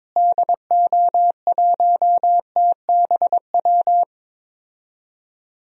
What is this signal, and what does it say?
Morse code, heard as a keyed tone, about 22 wpm, 700 Hz. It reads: DO1TBW
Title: DXCC-100 Call Signs - 6 Characters - Repeated 2x - 22wpm